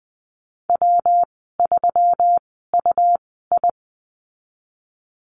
Morse code, heard as a keyed tone, about 20 wpm, spaced as for 15 wpm, 700 Hz. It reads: W3UI
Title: 128 Lesson 21g - Number 6 - Call Signs 20wpm